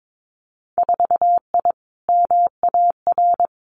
Morse code, heard as a keyed tone, about 22 wpm, 700 Hz. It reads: 4I MAR
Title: Winter Field Day Exchanges Encore 22wpm